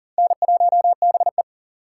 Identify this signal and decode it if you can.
Morse code, heard as a keyed tone, about 40 wpm, 700 Hz. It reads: N1BE
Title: USA Call Signs - 4 Characters - 4x Character Spacing - 40wpm